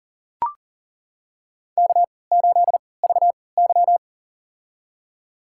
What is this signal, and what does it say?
Morse code, heard as a keyed tone, about 40 wpm, spaced as for 23 wpm, 700 Hz. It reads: K8VY
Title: USA Call Signs - 4 Characters - 3x Character Spacing - 40wpm